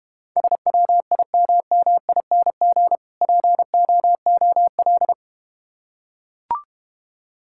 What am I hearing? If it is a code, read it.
Morse code, heard as a keyed tone, about 31 wpm, 700 Hz. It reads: SWIMMING POOL
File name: Easy Words - Speed-Racing with Hints - 25wpm